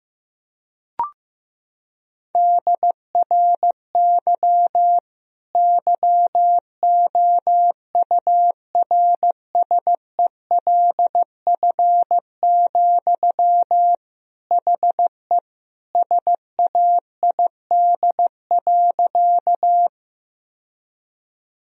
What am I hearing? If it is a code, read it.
Morse code, heard as a keyed tone, about 15 wpm, 700 Hz. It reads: DRY YOURSELF, HE SAID.
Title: Sentences from Top 1000 Words - Review Part 3 - 15wpm